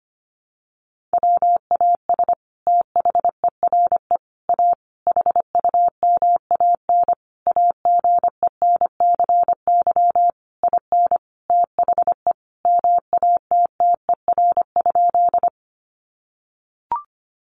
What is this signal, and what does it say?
Morse code, heard as a keyed tone, about 25 wpm, 700 Hz. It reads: WAS THERE A HUMAN AGENCY IN THE MATTER?
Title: Sentences from Top 1300 Words 25wpm